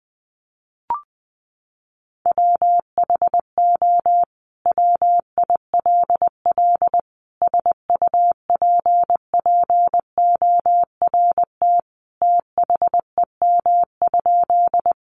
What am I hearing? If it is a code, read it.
Morse code, heard as a keyed tone, about 20 wpm, 700 Hz. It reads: WHO WILL SUPPORT THEM?
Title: Sentences from Top 800 Words 20wpm